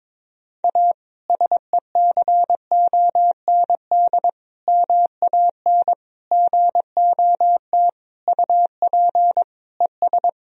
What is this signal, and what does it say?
Morse code, heard as a keyed tone, about 22 wpm, 700 Hz. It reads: A SECOND MAN GOT UP ES
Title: Sentences from Top 300 Words and Abbreviations 22wpm